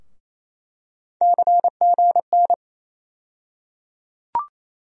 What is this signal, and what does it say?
Morse code, heard as a keyed tone, about 28 wpm, 700 Hz. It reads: CGN
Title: USA Call Sign Postfixes - 3 Letters 28wpm